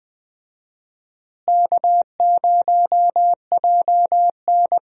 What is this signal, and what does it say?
Morse code, heard as a keyed tone, about 20 wpm, 700 Hz. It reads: K0JN
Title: USA Call Signs - 4 Characters - Repeated 3x - 20wpm